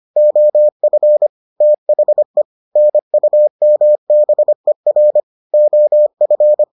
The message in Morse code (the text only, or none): OF THE NUMBER OF